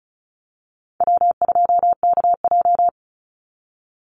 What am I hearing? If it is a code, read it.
Morse code, heard as a keyed tone, about 34 wpm, 700 Hz. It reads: W2KJ
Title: Easy Call Signs Repeated 3x - 35wpm